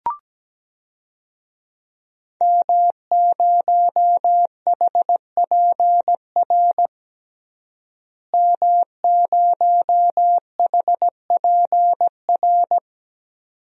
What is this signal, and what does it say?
Morse code, heard as a keyed tone, about 17 wpm, 700 Hz. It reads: M0HPR M0HPR
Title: DXCC-100 Call Signs - 5 Characters - Repeated 2x - 17wpm